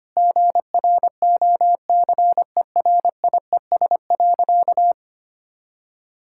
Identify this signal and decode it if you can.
Morse code, heard as a keyed tone, about 25 wpm, 700 Hz. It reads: GROCERIES.
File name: practice / 25wpm / Sentences with 3-Word Phrases - 25wpm